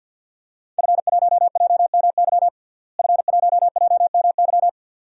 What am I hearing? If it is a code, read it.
Morse code, heard as a keyed tone, about 47 wpm, 700 Hz. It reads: F1JMY F1JMY
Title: DXCC-100 Call Signs - 5 Characters - Repeated 2x - 50wpm